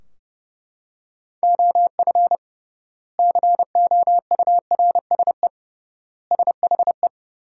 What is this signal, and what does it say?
Morse code, heard as a keyed tone, about 30 wpm, 700 Hz. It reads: OF COURSE SHE
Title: 3x Word-Spacing - Sentences from Top 500 Words - 30wpm